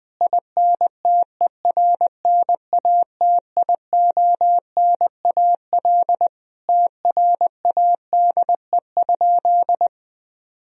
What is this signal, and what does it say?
Morse code, heard as a keyed tone, about 21 wpm, 700 Hz. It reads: INTERNATIONAL TRADE?
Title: Sentences from Top 1000 Words - Review Part 4 - 20wpm